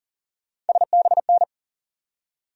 Morse code, heard as a keyed tone, about 40 wpm, 700 Hz. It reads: SBN
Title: USA Call Sign Postfixes - 3 Letters 40wpm